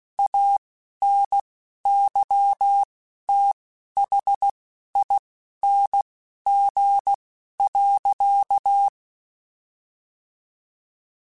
Morse code, heard as a keyed tone, about 16 wpm, spaced as for 11 wpm, 800 Hz. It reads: ANYTHING.
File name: Quotes / Quotes-mp3s / StandFor @10